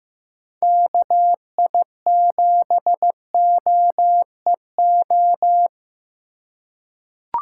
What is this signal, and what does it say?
Morse code, heard as a keyed tone, about 15 wpm, 700 Hz. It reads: KI7OEO
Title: USA Call Signs - 6 Characters - 3x Character Spacing - 15wpm